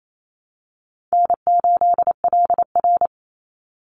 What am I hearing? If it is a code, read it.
Morse code, heard as a keyed tone, about 28 wpm, 700 Hz. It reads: N8LR
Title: Easy Call Signs - 28wpm